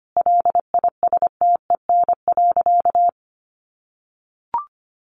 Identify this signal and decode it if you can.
Morse code, heard as a keyed tone, about 25 wpm, 700 Hz. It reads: LISTEN.
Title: Sentences from Top 900 Words 25wpm